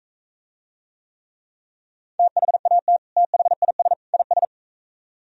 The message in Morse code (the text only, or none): THAT THIS IS